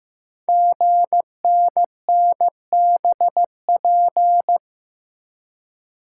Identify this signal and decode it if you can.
Morse code, heard as a keyed tone, about 15 wpm, 700 Hz. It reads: GNNBP
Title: Code Groups - 5 Letters - 15wpm